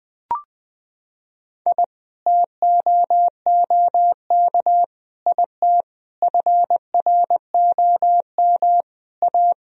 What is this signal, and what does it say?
Morse code, heard as a keyed tone, about 20 wpm, 700 Hz. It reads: I TOOK IT FROM A
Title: Sentences from Top 1000 Words - Review Part 1 - 20wpm